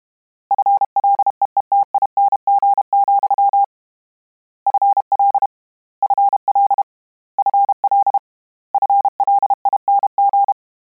Morse code, heard as a keyed tone, about 32 wpm, 800 Hz. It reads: FLEETING, FL FL FL FLING
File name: Combinations / Combinations-mp3s / Combinations@30